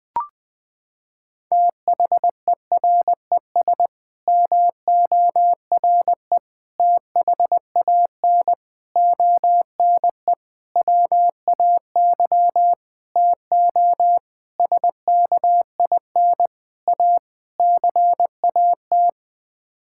Morse code, heard as a keyed tone, about 21 wpm, 700 Hz. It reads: THERES MORE THAN ONE WAY TO SKIN A CAT
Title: Sets of 9 Words - 20wpm